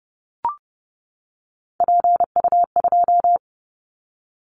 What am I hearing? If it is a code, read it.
Morse code, heard as a keyed tone, about 30 wpm, 700 Hz. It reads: PU2
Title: Callsign Prefixes for Top 100 DXCC - 30wpm